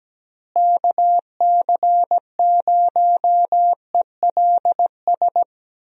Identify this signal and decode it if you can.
Morse code, heard as a keyed tone, about 17 wpm, 700 Hz. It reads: KC0ELS
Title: USA Call Signs - 6 Characters - Repeated 3x - 17wpm